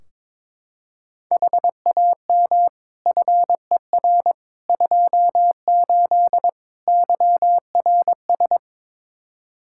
Morse code, heard as a keyed tone, about 22 wpm, 700 Hz. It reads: HAM FER 28 YRS
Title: Practice QSO - Exchange 3 - More Info - 22wpm